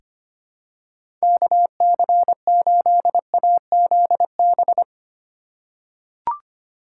Morse code, heard as a keyed tone, about 25 wpm, 700 Hz. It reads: KC8AZB
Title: USA Call Signs - 6 Characters - 2x Character Spacing - 25wpm